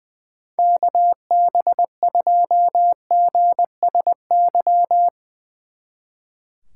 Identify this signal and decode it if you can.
Morse code, heard as a keyed tone, about 20 wpm, 700 Hz. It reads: KB2GSY
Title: Speed Racing - USA 2x3 Call Signs 20wpm